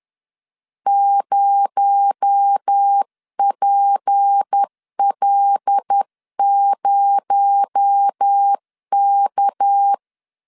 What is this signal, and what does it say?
Morse code, heard as a keyed tone, about 10 wpm, 800 Hz. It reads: 0PL0K
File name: LEZIONI / 7 GF9Z / 7 GF 10wpm 2514